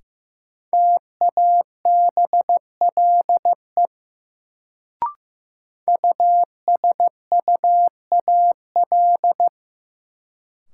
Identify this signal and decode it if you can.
Morse code, heard as a keyed tone, about 15 wpm, 700 Hz. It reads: TABLE USUAL
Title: Sentences from Top 400 Words - New Word Drill - 15wpm